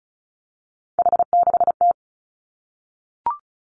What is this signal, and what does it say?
Morse code, heard as a keyed tone, about 36 wpm, 700 Hz. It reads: H6T
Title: Three Challenging Letter-Number Combinations 35wpm